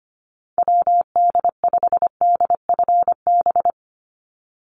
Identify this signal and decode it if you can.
Morse code, heard as a keyed tone, about 25 wpm, 690 Hz. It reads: WD5DFB
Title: USA Call Signs 25wpm